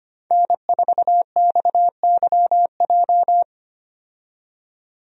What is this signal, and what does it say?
Morse code, heard as a keyed tone, about 25 wpm, 700 Hz. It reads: N4XYJ
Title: USA Call Signs - 5 Characters - 25wpm